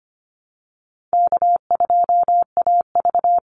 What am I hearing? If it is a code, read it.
Morse code, heard as a keyed tone, about 25 wpm, 700 Hz. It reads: K2AV
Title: Call Signs - 25wpm